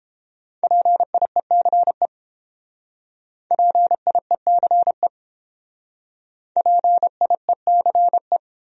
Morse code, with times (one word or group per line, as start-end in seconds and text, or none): 0.6-2.1 PIECE
3.5-5.1 PIECE
6.6-8.4 PIECE